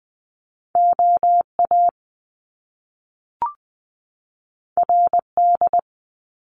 Morse code, heard as a keyed tone, about 20 wpm, 700 Hz. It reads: OA RD
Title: Top Two Letter Combinations in Words 20wpm